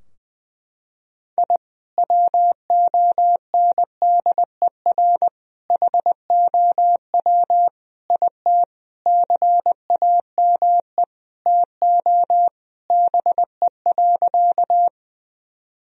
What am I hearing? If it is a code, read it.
Morse code, heard as a keyed tone, about 20 wpm, 700 Hz. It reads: I WONDER HOW IT CAME TO BE.
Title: Sentences from Top 1000 Words - Review Part 3 - 20wpm